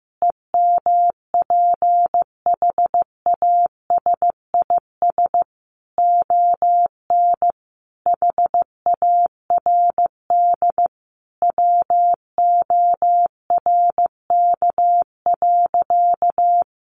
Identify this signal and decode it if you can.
Morse code, heard as a keyed tone, about 14 wpm, 700 Hz. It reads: EMPHASIS ON HARD WORK.
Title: Sentences from Top 1400 Words 15wpm